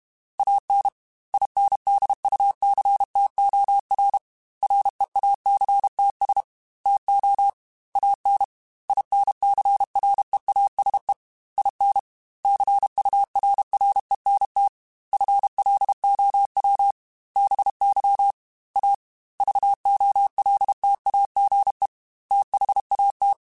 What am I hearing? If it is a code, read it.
Morse code, heard as a keyed tone, about 32 wpm, 800 Hz. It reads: AN INDUCTOR REACTS TO AN INCREASE IN CURRENT FLOW BY A VOLTAGE THAT
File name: Technical Subjects / Technical Subjects-mp3s / XL @30